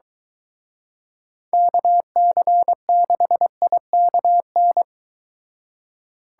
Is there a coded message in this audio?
Yes: Morse code from 1.5 to 4.8 s, heard as a keyed tone, about 22 wpm, 700 Hz. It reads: KC6IKN